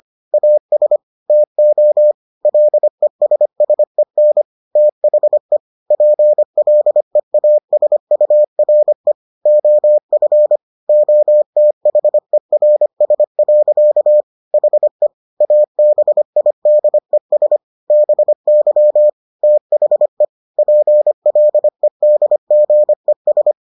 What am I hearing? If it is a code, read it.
Morse code, heard as a keyed tone, about 25 wpm, 600 Hz. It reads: AS TO LESSEN THE PLEASURE OF OTHERS. HE ABIDES BY THE PLEDGES